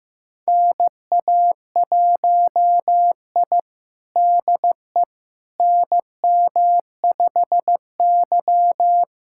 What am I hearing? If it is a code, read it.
Morse code, heard as a keyed tone, about 15 wpm, 700 Hz. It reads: NA1I DE NM5Y
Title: USA Callsign DE Callsign - 4 Characters - 15wpm